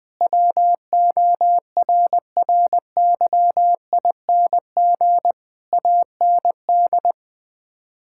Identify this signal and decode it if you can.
Morse code, heard as a keyed tone, about 20 wpm, 700 Hz. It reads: WORRYING AND
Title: Sets of 8 Words - 20wpm